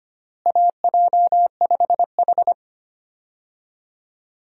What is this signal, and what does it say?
Morse code, heard as a keyed tone, about 25 wpm, 700 Hz. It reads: AJ5H
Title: USA Call Signs - 4 Characters - Repeated 2x - 25wpm